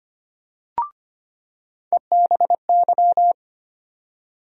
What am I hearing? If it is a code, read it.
Morse code, heard as a keyed tone, about 25 wpm, 700 Hz. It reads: EBY
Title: USA Call Sign Postfixes - 3 Letters 25wpm